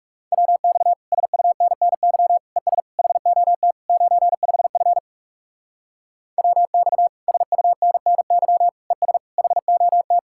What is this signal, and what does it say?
Morse code, heard as a keyed tone, about 36 wpm, 700 Hz. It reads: WX SUNNY ES HOT 95F WX SUNNY ES HOT
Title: Speed-Racing Common QSO Phrases - 30wpm